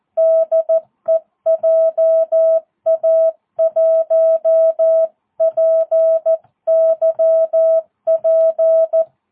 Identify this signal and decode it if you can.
Morse code, heard as a keyed tone, about 14 wpm, 650 Hz. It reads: DE JA1PYP
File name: jh8cbh / 2022cw